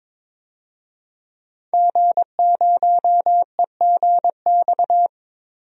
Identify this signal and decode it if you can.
Morse code, heard as a keyed tone, about 22 wpm, 700 Hz. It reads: G0EGX
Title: DXCC-100 Call Signs - 5 Characters - 22wpm